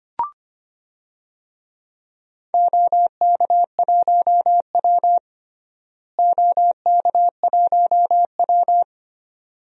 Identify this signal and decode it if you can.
Morse code, heard as a keyed tone, about 25 wpm, 700 Hz. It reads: OK1W OK1W
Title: DXCC-100 Call Signs - 4 Characters - Repeated 2x - 25wpm